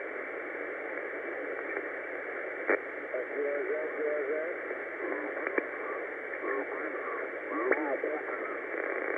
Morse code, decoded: none